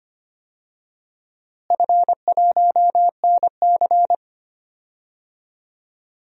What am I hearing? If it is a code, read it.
Morse code, heard as a keyed tone, about 25 wpm, 700 Hz. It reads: F1NC
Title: DXCC-100 Call Signs - 4 Characters - Repeated 2x - 25wpm